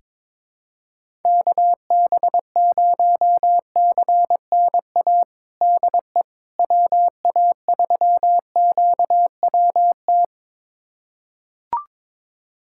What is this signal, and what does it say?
Morse code, heard as a keyed tone, about 22 wpm, 700 Hz. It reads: KB0CNA DE WA3QWT
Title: USA Callsign DE Callsign - 6 Characters - 22wpm